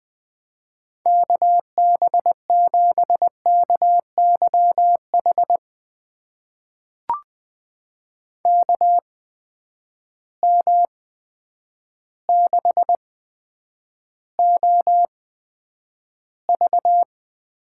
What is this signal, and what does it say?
Morse code, heard as a keyed tone, about 21 wpm, spaced as for 9 wpm, 700 Hz. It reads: KB7KYH KM6OV
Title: USA Call Signs - 6 Characters - 8x Character Spacing - 20wpm